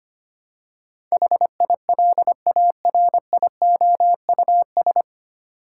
Morse code, heard as a keyed tone, about 25 wpm, 700 Hz. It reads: HILARIOUS
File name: Nine Letter Words - 25wpm